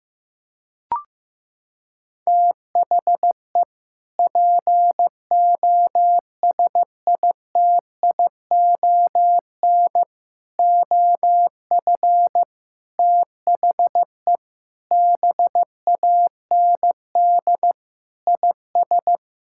Morse code, heard as a keyed tone, about 14 wpm, 700 Hz. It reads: THE POSITION OF THE BAND IS
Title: Sentences from Top 1600 Words 15wpm